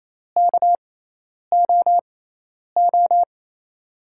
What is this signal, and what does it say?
Morse code, heard as a keyed tone, about 28 wpm, spaced as for 10 wpm, 700 Hz. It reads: KOO
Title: USA Call Signs - 5 Characters - 6x Character Spacing - 28wpm